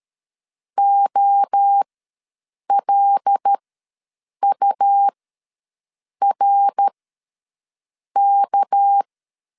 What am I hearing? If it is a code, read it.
Morse code, heard as a keyed tone, about 13 wpm, spaced as for 6 wpm, 800 Hz. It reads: OLURK
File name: LEZIONI / 8 YJ45 / 8 Y ws 12wpm 3409